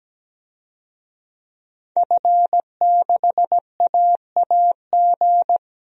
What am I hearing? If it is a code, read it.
Morse code, heard as a keyed tone, about 17 wpm, 700 Hz. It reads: F6AAG
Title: DXCC-100 Call Signs - 5 Characters - 17wpm